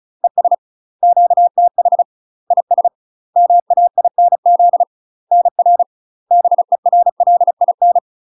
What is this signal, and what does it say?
Morse code, heard as a keyed tone, about 36 wpm, 700 Hz. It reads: ES QTH IS MAINZ NR BERLIN
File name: UK MASTER QSOs TRAINING/35WPM